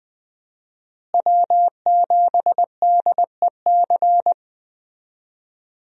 Morse code, heard as a keyed tone, about 20 wpm, 700 Hz. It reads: W7DEC
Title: USA Call Signs - 5 Characters - Repeated 2x - 20wpm